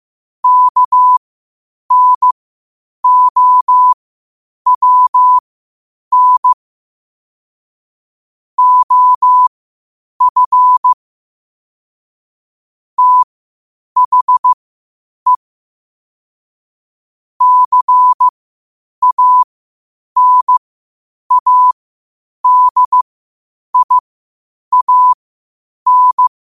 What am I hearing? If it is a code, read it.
Morse code, heard as a keyed tone, about 15 wpm, spaced as for 8 wpm, 1000 Hz. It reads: KNOWN OF THE CANADIAN
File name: 8wpm